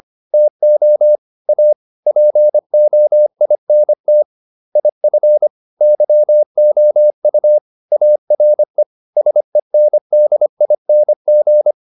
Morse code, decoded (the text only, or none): TO A POINT IF YOU ARE SENDING